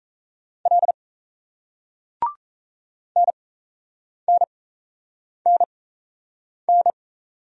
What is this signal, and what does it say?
Morse code, heard as a keyed tone, about 36 wpm, spaced as for 8 wpm, 700 Hz. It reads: L NNNN